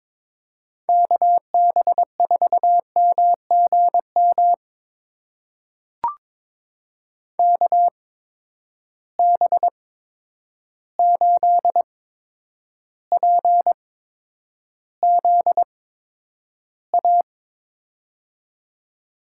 Morse code, heard as a keyed tone, about 22 wpm, spaced as for 9 wpm, 700 Hz. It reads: KB4MGM KB8PZA
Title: USA Call Signs - 6 Characters - 8x Character Spacing - 22wpm